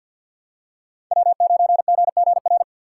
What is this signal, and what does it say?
Morse code, heard as a keyed tone, about 52 wpm, 690 Hz. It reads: W9GGR
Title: USA Call Signs - 2x Character Spacing - 50wpm